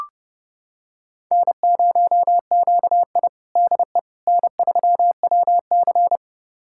Morse code, heard as a keyed tone, about 30 wpm, 700 Hz. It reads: N0QI DE N3WC
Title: USA Callsign DE Callsign - 4 Characters - 30wpm